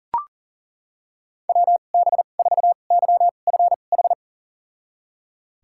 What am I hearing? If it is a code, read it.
Morse code, heard as a keyed tone, about 37 wpm, spaced as for 28 wpm, 700 Hz. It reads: WB4YFH
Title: USA Call Signs - 6 Characters - 2x Character Spacing - 40wpm